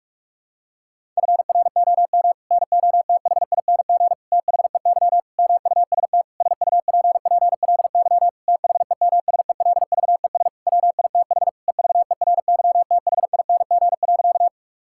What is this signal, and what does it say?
Morse code, heard as a keyed tone, about 46 wpm, 700 Hz. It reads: FROM NOTHING THEY MUST SUPPLY THEMSELVES WITH EVERYTHING.
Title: Sentences from Top 1000 Words - Review Part 5 - 45wpm